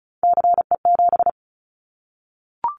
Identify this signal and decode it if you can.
Morse code, heard as a keyed tone, about 34 wpm, 700 Hz. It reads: CE7
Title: Callsign Prefixes for Top 100 DXCC - 35wpm